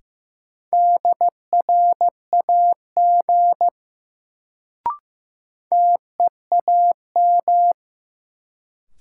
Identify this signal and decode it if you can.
Morse code, heard as a keyed tone, about 15 wpm, 700 Hz. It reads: DRAG TEAM
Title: Sentences from Top 1500 Words - New Word Drill Alt - 15wpm